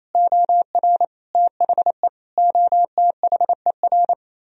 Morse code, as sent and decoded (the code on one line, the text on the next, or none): --- .-. / - .... / --- - .... .-.
OR THE OTHER